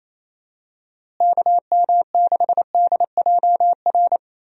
Morse code, heard as a keyed tone, about 27 wpm, 700 Hz. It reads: KM6DJR
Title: USA Call Signs - 6 Characters - Repeated 3x - 28wpm